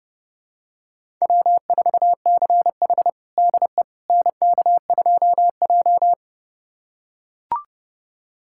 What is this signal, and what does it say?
Morse code, heard as a keyed tone, about 32 wpm, 700 Hz. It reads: W4CH DE NK2J
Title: USA Callsign DE Callsign - 4 Characters - 30wpm